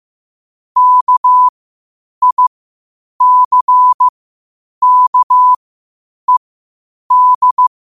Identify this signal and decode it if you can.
Morse code, heard as a keyed tone, about 15 wpm, spaced as for 8 wpm, 1000 Hz. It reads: KICKED
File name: qotdfiles / 8wpm